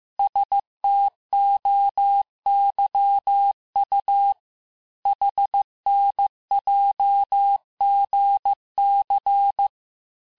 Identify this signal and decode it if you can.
Morse code, heard as a keyed tone, about 15 wpm, 800 Hz. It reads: STOYU HNJGC
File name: cours-du-lundi files / cours-1585-texte-chiffre-vitesse-14mpm